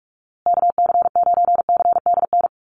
Morse code, heard as a keyed tone, about 44 wpm, 700 Hz. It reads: KC9CDN
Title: USA Call Signs Repeated 2x 45wpm